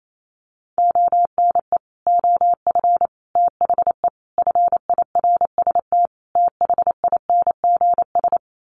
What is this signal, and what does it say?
Morse code, heard as a keyed tone, about 28 wpm, 700 Hz. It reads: ONE OF THE FIRST THINGS
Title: Sets of 5 Words Encore 28wpm